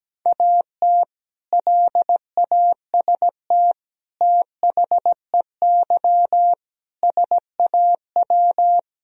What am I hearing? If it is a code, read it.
Morse code, heard as a keyed tone, about 17 wpm, 700 Hz. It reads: AT LAST THEY SAW
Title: Sentences from Top 1000 Words - Review Part 1 - 17wpm